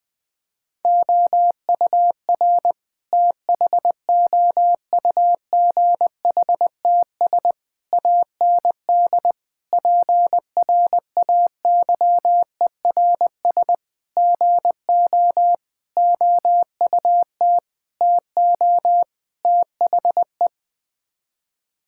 Morse code, heard as a keyed tone, about 20 wpm, 700 Hz. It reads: OUR THOUGHTS AND PRAYERS GO OUT TO THE
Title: Sets of 8 Words - 20wpm